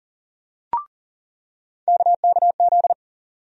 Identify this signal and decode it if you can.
Morse code, heard as a keyed tone, about 40 wpm, 700 Hz. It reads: KKZ
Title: USA Call Sign Postfixes - 3 Letters 40wpm